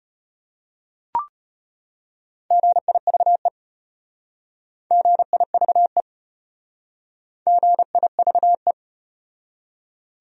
Morse code, heard as a keyed tone, about 32 wpm, 700 Hz. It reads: GIVE GIVE GIVE